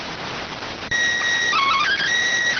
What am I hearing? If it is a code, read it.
SSTV (slow-scan television), PD 290.